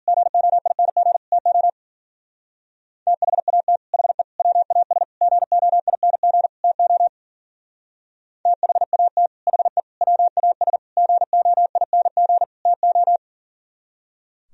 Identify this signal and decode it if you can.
Morse code, heard as a keyed tone, about 50 wpm, 700 Hz. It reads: GOING TO THAT HE WAS GOING TO THAT HE WAS GOING TO
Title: Speed-Racing - Sets of 5 Words 40wpm